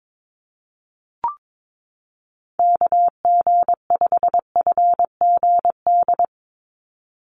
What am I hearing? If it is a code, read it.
Morse code, heard as a keyed tone, about 22 wpm, 700 Hz. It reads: KG5FGD
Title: USA Call Signs 22wpm